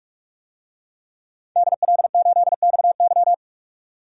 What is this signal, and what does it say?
Morse code, heard as a keyed tone, about 44 wpm, 690 Hz. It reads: DL8XY
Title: DXCC-100 Call Signs - 5 Characters - 45wpm